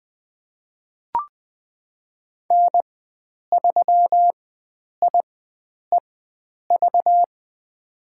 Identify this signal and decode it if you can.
Morse code, heard as a keyed tone, about 20 wpm, spaced as for 9 wpm, 700 Hz. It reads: N3IEV
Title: USA Call Signs - 5 Characters - 4x Character Spacing - 20wpm